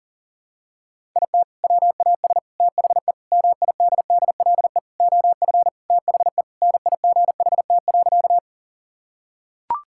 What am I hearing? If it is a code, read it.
Morse code, heard as a keyed tone, about 40 wpm, 700 Hz. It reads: IT WAS THE MIDDLE OF THE NIGHT.
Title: Sentences from Top 1000 Words - Review Part 6 - 40wpm